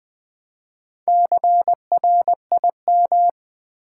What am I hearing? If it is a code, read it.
Morse code, heard as a keyed tone, about 20 wpm, 700 Hz. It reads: CRIM
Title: Build A Word - 4 Letters At A Time - 8-Letter Words - 20wpm